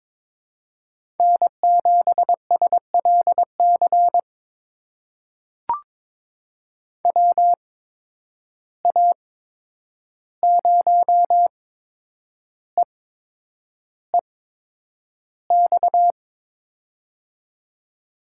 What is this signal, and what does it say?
Morse code, heard as a keyed tone, about 22 wpm, spaced as for 9 wpm, 700 Hz. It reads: N7SLC WA0EEX
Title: USA Call Signs - 8x Character Spacing - 22wpm